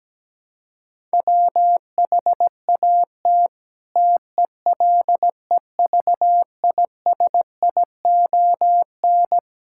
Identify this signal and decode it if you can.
Morse code, heard as a keyed tone, about 17 wpm, 700 Hz. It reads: WHAT TELEVISION
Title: Sentences from Top 1000 Words - Review Part 5 - 17wpm